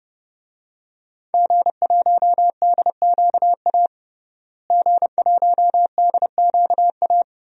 Morse code, heard as a keyed tone, about 32 wpm, 700 Hz. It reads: G1DQA G1DQA
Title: DXCC-100 Call Signs - 5 Characters - Repeated 2x - 30wpm